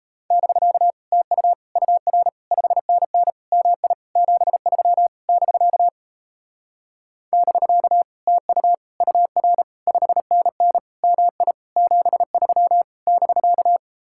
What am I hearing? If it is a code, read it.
Morse code, heard as a keyed tone, about 38 wpm, 700 Hz. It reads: <BK> TU UR 5NN MI 73 <BK> <BK> TU UR 5NN MI 73 <BK>